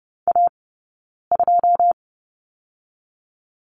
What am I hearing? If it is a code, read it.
Morse code, heard as a keyed tone, about 30 wpm, 700 Hz. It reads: A 2
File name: Lesson 17d - Letter u - Two Characters 30wpm